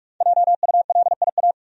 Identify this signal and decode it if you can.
Morse code, heard as a keyed tone, about 45 wpm, 700 Hz. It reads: JULIA
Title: USA Common Names - 45wpm